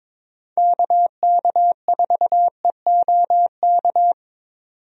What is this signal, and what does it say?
Morse code, heard as a keyed tone, about 21 wpm, 700 Hz. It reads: KK4EOK
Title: USA Call Signs - 6 Characters - 22wpm